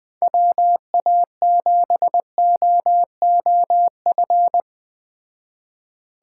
Morse code, heard as a keyed tone, about 20 wpm, 700 Hz. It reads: WA7OOF